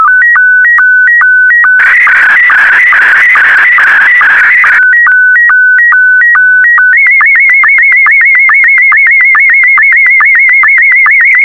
SSTV (slow-scan television), Scottie 1.